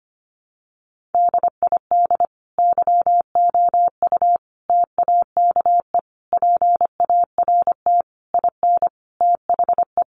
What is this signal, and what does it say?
Morse code, heard as a keyed tone, about 25 wpm, 700 Hz. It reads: DID YOU TAKE PART IN THE
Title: Sentences from Top 1200 Words 25wpm